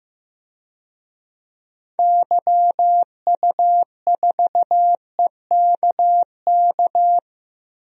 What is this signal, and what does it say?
Morse code, heard as a keyed tone, about 15 wpm, 700 Hz. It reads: YU4EKK
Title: DXCC-100 Call Signs - 6 Characters - 15wpm